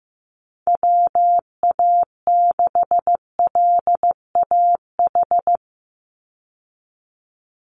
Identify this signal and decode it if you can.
Morse code, heard as a keyed tone, about 15 wpm, 700 Hz. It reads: WA6LAH
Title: USA Call Signs 15wpm